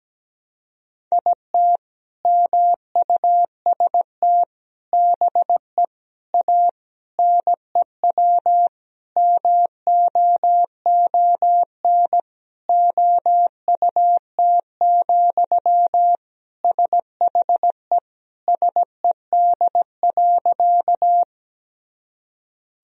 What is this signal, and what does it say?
Morse code, heard as a keyed tone, about 17 wpm, 700 Hz. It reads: IT MUST BE A NEW MOON OUT, SHE SED.
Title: Sentences from Top 500 Words and Abbreviations 17wpm